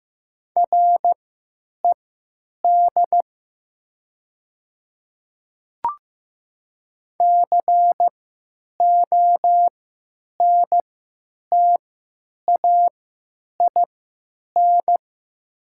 Morse code, heard as a keyed tone, about 15 wpm, spaced as for 8 wpm, 700 Hz. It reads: RED CONTAIN
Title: 3x Character Spacing - Top 500 Words - 15wpm